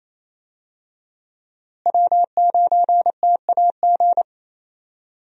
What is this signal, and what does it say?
Morse code, heard as a keyed tone, about 27 wpm, 700 Hz. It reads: W9TAG
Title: USA Call Signs - 5 Characters - Repeated 3x - 28wpm